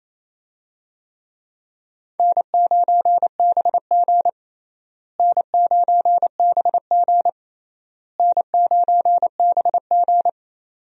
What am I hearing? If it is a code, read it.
Morse code, heard as a keyed tone, about 27 wpm, 700 Hz. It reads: N9BG N9BG N9BG
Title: USA Call Signs - 4 Characters - Repeated 3x - 28wpm